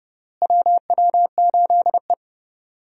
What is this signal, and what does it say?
Morse code, heard as a keyed tone, about 30 wpm, 700 Hz. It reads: WW8E
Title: USA Call Signs - 4 Characters - Repeated 3x - 30wpm